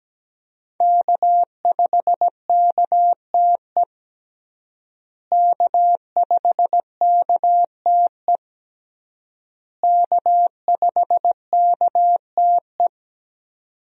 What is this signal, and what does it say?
Morse code, heard as a keyed tone, about 17 wpm, 700 Hz. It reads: K5KTE K5KTE K5KTE
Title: USA Call Signs - 5 Characters - Repeated 3x - 17wpm